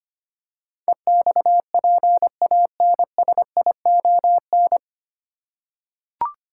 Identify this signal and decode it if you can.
Morse code, heard as a keyed tone, about 25 wpm, 700 Hz. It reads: EXPANSION
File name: practice / 25wpm / 9 Letter Words - First 3 Letters Given - 25wpm